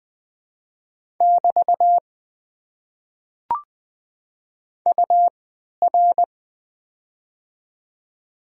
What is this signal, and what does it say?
Morse code, heard as a keyed tone, about 20 wpm, spaced as for 11 wpm, 690 Hz. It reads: <BT> UR
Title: Top 90 QSO Elements - 3x Character Spacing - 20wpm